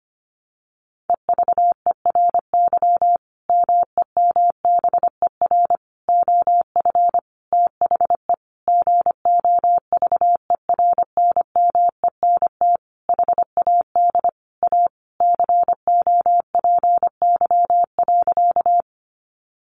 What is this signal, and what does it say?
Morse code, heard as a keyed tone, about 25 wpm, 700 Hz. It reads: EVERY MEMBER OF THE GOVERNMENT HAD A COPY.
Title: Sentences from Top 1400 Words 25wpm